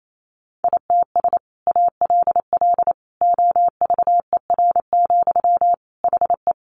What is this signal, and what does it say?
Morse code, heard as a keyed tone, about 28 wpm, 700 Hz. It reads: ITS ALL OVER, HE
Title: Sentences from Top 600 Words 28wpm